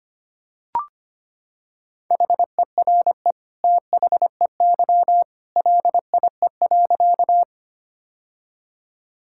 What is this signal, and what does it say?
Morse code, heard as a keyed tone, about 25 wpm, 700 Hz. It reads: HERE THEY LIE.
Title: Sentences from Top 1600 Words 25wpm